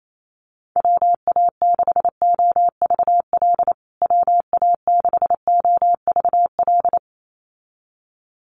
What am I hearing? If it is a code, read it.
Morse code, heard as a keyed tone, about 28 wpm, 700 Hz. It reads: WA6OVL WA6OVL
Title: USA Call Signs Repeated 2x 28wpm